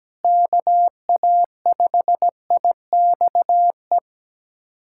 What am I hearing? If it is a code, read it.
Morse code, heard as a keyed tone, about 17 wpm, 700 Hz. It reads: KA5IXE
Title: USA Callsign DE Callsign - 6 Characters - 17wpm